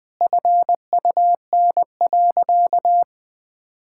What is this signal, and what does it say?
Morse code, heard as a keyed tone, about 20 wpm, 700 Hz. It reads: FUN.
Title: Sentences with 4-Word Phrases - 20wpm